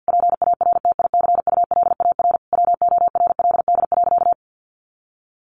Morse code, heard as a keyed tone, about 50 wpm, 700 Hz. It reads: PARTICULAR WORLD.
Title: Sentences from Top 800 Words 50wpm